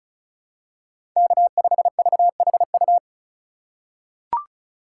Morse code, heard as a keyed tone, about 36 wpm, 700 Hz. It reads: K5VHU
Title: USA Call Signs - 5 Characters - 3x Character Spacing - 35wpm